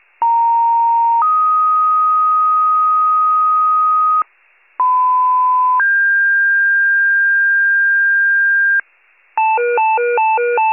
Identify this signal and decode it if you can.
Two-tone page, heard as a one-second tone, then a three-second tone, about 930 then 1250 Hz, about 1010 then 1600 Hz, with noise. It is heard from 0.2 to 8.8 s.
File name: wcalert